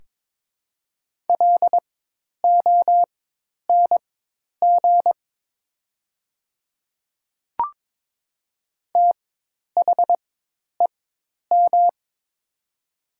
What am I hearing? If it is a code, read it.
Morse code, heard as a keyed tone, about 22 wpm, spaced as for 10 wpm, 700 Hz. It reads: LONG THEM
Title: 4x Character Spacing - Top 100 Words - 22wpm